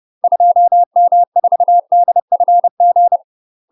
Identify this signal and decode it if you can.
Morse code, heard as a keyed tone, about 30 wpm, 700 Hz. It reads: 2M4DFG
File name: UK MASTER QSOs TRAINING/30WPM